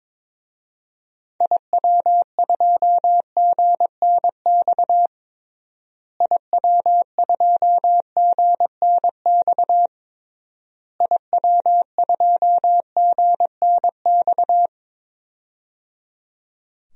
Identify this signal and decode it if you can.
Morse code, heard as a keyed tone, about 22 wpm, 700 Hz. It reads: IW2GNX IW2GNX IW2GNX
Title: DXCC-100 Call Signs - 6 Characters - Repeated 3x - 22wpm